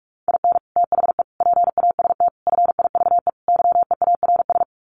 Morse code, heard as a keyed tone, about 44 wpm, 700 Hz. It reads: IN THE PAST FIVE YEARS
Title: Sets of 5 Words Encore 45wpm